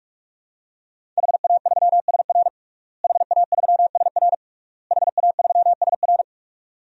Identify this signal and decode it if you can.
Morse code, heard as a keyed tone, about 46 wpm, 700 Hz. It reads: HA3SR HA3SR HA3SR
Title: DXCC-100 Call Signs - 5 Characters - Repeated 3x - 45wpm